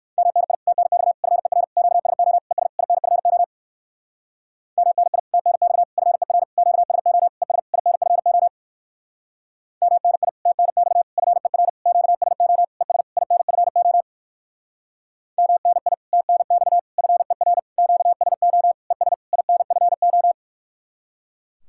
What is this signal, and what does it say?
Morse code, heard as a keyed tone, about 56 wpm, 700 Hz. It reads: MNI TNX FER QSO ES INFO MNI TNX FER QSO ES INFO MNI TNX FER QSO ES INFO MNI TNX FER QSO ES INFO
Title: Speed-Racing Common QSO Phrases - 45wpm